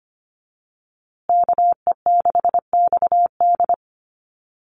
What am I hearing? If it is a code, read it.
Morse code, heard as a keyed tone, about 25 wpm, 700 Hz. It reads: KE6XD
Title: USA Call Signs Repeated 3x 25wpm